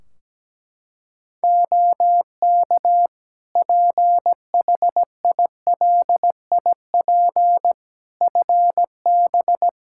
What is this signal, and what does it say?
Morse code, heard as a keyed tone, about 17 wpm, 700 Hz. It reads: OK PHILIP FB
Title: K1USN SST QSO Practice 17wpm